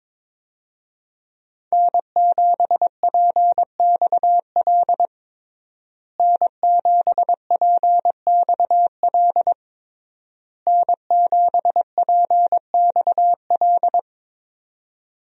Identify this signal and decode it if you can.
Morse code, heard as a keyed tone, about 22 wpm, 700 Hz. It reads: N7PXL N7PXL N7PXL
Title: USA Call Signs - 5 Characters - Repeated 3x - 22wpm